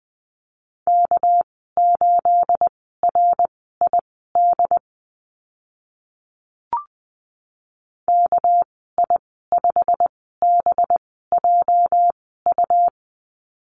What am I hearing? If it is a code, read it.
Morse code, heard as a keyed tone, about 20 wpm, spaced as for 14 wpm, 700 Hz. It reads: K8RID KI5BJU
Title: 219 Lesson 35g - Number 0 - Call Signs 20wpm